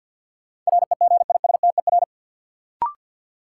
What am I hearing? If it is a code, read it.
Morse code, heard as a keyed tone, about 53 wpm, 700 Hz. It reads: REGISTER
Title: Sentences from Top 1200 Words - New Word Drill - 50wpm